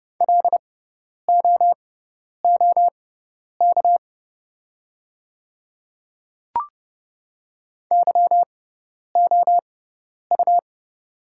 Morse code, heard as a keyed tone, about 29 wpm, spaced as for 10 wpm, 700 Hz. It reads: LOOK YOU
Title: 6x Character Spacing - Top 100 Words - 30wpm